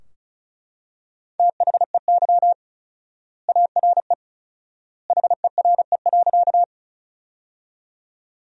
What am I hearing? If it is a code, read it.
Morse code, heard as a keyed tone, about 36 wpm, 700 Hz. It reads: THEY ARE HERE.
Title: 4x Word-Spacing - Sentences from Top 200 Words - 35wpm